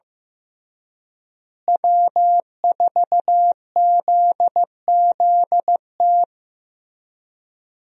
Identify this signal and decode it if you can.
Morse code, heard as a keyed tone, about 16 wpm, 700 Hz. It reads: W4ZZT
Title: USA Call Signs - 5 Characters - Repeated 3x - 15wpm